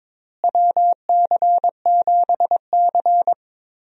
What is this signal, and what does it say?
Morse code, heard as a keyed tone, about 22 wpm, 700 Hz. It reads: WC7C
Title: USA Call Signs - 4 Characters - Repeated 3x - 22wpm